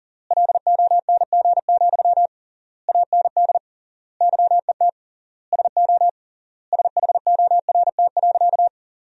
Morse code, heard as a keyed tone, about 40 wpm, 700 Hz. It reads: LONG, AND YET SO SHORT.
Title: 3x Word-Spacing - Sentences from Top 500 Words - 40wpm